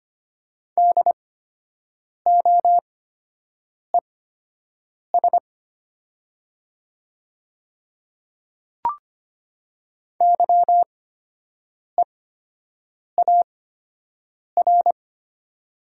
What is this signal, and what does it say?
Morse code, heard as a keyed tone, about 25 wpm, spaced as for 7 wpm, 700 Hz. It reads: DOES YEAR
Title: 8x Character Spacing - Top 500 Words - 25wpm